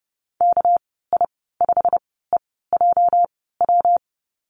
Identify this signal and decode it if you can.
Morse code, heard as a keyed tone, about 30 wpm, spaced as for 17 wpm, 700 Hz. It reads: KI5EJW
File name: Lesson 37g - Letter x - Call Signs 30wpm